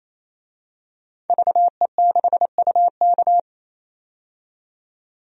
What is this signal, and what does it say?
Morse code, heard as a keyed tone, about 28 wpm, 700 Hz. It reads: VE6UK
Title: DXCC-100 Call Signs - 5 Characters - 28wpm